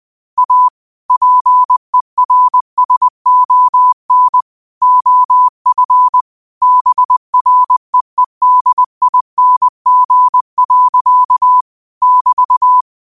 Morse code, heard as a keyed tone, about 20 wpm, 1000 Hz. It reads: A PERSON OF BREEDING. <BT>
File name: qotdfiles / 20wpm